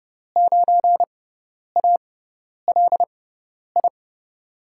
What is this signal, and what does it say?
Morse code, heard as a keyed tone, about 30 wpm, spaced as for 10 wpm, 700 Hz. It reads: 9ALI
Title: USA Call Signs - 5 Characters - 6x Character Spacing - 30wpm